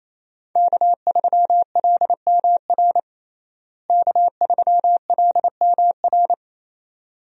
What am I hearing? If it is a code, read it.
Morse code, heard as a keyed tone, about 28 wpm, 700 Hz. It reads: K3LMR K3LMR
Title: USA Call Signs - 5 Characters - Repeated 2x - 28wpm